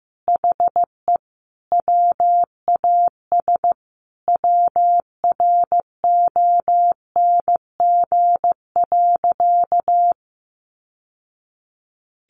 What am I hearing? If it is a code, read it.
Morse code, heard as a keyed tone, about 15 wpm, 700 Hz. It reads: HE WAS WRONG.
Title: Sentences from Top 1100 Words 15wpm